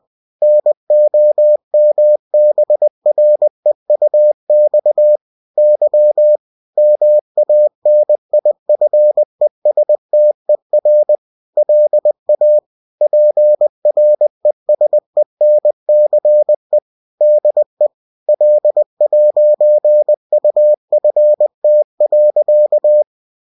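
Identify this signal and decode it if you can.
Morse code, heard as a keyed tone, about 20 wpm, 600 Hz. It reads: NOMBREUX Y MANIFESTER LA PRESENCE DE L'UFT.
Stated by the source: UFT bulletin sent in CW.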